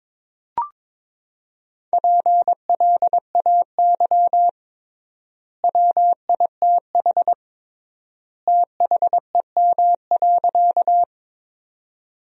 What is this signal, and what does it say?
Morse code, heard as a keyed tone, about 22 wpm, 700 Hz. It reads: PLAY WITH THEM.
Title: 3x Word-Spacing - Sentences from Top 200 Words - 22wpm